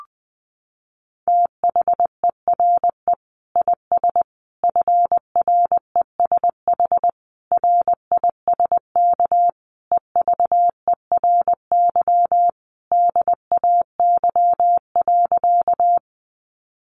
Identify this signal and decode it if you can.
Morse code, heard as a keyed tone, about 20 wpm, 700 Hz. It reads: THERE IS FRESH RISK EVERY DAY.
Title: Sentences from Top 1200 Words 20wpm